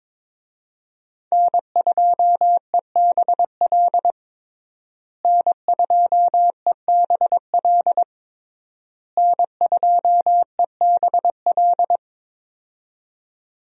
Morse code, heard as a keyed tone, about 22 wpm, 700 Hz. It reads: N2EBL N2EBL N2EBL
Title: USA Call Signs - 5 Characters - Repeated 3x - 22wpm